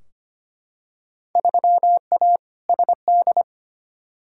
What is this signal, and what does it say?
Morse code, heard as a keyed tone, about 25 wpm, 700 Hz. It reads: 3A SD
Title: Field Day Exchanges - 25wpm